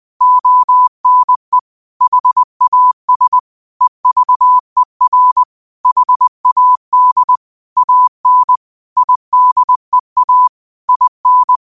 Morse code, heard as a keyed tone, about 20 wpm, 1000 Hz. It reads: ONE HAS EVER HAD AN IDEA IN
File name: qotdfiles / 20wpm